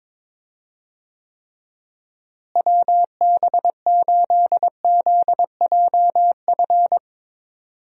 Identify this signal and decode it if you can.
Morse code, heard as a keyed tone, about 22 wpm, 700 Hz. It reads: WB8ZJF
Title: USA Call Signs - 6 Characters - Repeated 3x - 22wpm